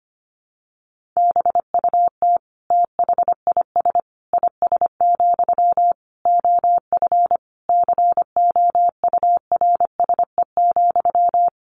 Morse code, heard as a keyed tone, about 25 wpm, 700 Hz. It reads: BUT THIS IS, OF COURSE,
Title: Sentences from Top 1500 Words 25wpm